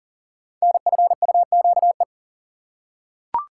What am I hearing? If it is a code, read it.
Morse code, heard as a keyed tone, about 38 wpm, 700 Hz. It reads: NFUQE
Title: Code Groups - 5 Letter:Numbers - 40wpm